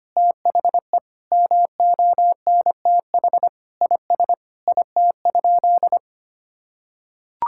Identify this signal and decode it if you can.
Morse code, heard as a keyed tone, about 25 wpm, 700 Hz. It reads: THE MONTH IS IT?
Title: Sentences from Top 1000 Words - Review Part 5 - 25wpm